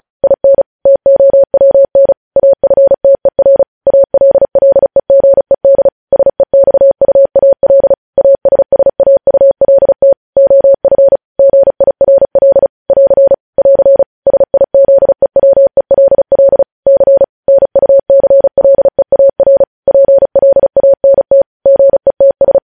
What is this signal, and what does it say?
Morse code, heard as a keyed tone, about 36 wpm, 550 Hz. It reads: IN TOWN AFTER ALLEGED SEXUAL ASSAULT OF GIRL <AR> <AR> SIZEWELL C NUCLEAR PLANT GETS
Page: news headlines in morse code at 35 wpm